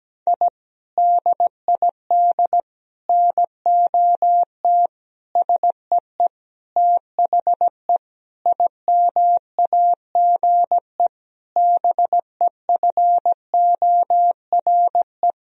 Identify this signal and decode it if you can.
Morse code, heard as a keyed tone, about 17 wpm, 700 Hz. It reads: I DID NOT SEE THE IMAGE BEFORE
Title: Sentences from Top 1000 Words - Review Part 4 - 17wpm